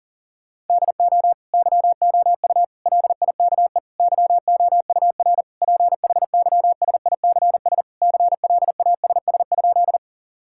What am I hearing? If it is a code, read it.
Morse code, heard as a keyed tone, about 40 wpm, 700 Hz. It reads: DO YOU LIKE YOUR PHYSICS CLASS?
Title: Sentences from Top 1000 Words - Review Part 5 - 40wpm